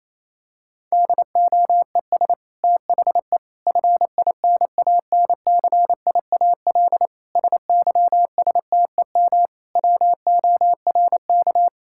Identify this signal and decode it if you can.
Morse code, heard as a keyed tone, about 27 wpm, 700 Hz. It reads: DOES THE FINANCIAL SYSTEM WORK
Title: Sentences from Top 1000 Words - Review Part 5 - 28wpm